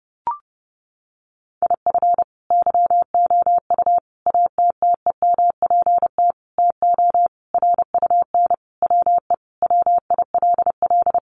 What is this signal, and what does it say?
Morse code, heard as a keyed tone, about 30 wpm, 700 Hz. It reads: IF YOU ATTEMPT TO RUN WE WILL
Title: Sentences from Top 1200 Words 30wpm